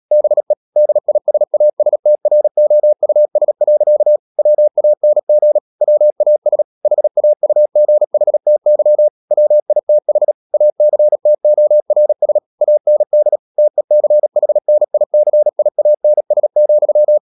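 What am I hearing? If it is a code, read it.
Morse code, heard as a keyed tone, about 37 wpm, 600 Hz. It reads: BE DISASTROUS. WANG WAS HAUGHTY WITH ACTORS AND TECHNICIANS,
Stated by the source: Morse code practice at 35–39 wpm sending English text of Ken Follett's novel Never, from CW med Gnister.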